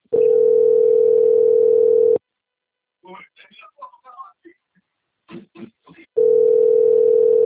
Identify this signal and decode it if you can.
Ringback tone (North America), 440 + 480 Hz, with speech.
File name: out- - - - - .44334